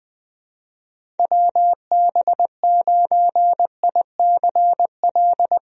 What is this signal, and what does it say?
Morse code, heard as a keyed tone, about 20 wpm, 700 Hz. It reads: WB9ICL
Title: USA Call Signs - 6 Characters - 20wpm